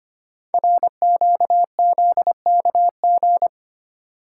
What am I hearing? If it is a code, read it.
Morse code, heard as a keyed tone, about 25 wpm, 700 Hz. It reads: RQZKG
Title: Code Groups - 5 Letter:Numbers - 25wpm